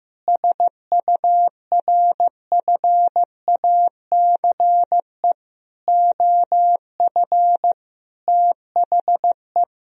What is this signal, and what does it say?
Morse code, heard as a keyed tone, about 15 wpm, 700 Hz. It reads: SURFACE OF THE
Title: Sentences from Top 1000 Words - Review Part 3 - 15wpm